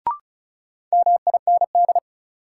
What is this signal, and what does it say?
Morse code, heard as a keyed tone, about 36 wpm, 700 Hz. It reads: MIND
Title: Sets of 5 Words - New Word Drill - 35wpm